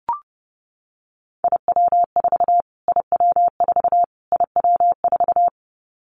Morse code, heard as a keyed tone, about 30 wpm, 700 Hz. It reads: IW4 IW4 IW4
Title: Callsign Prefixes for Top 25 DXCC - Repeated 3x - 30wpm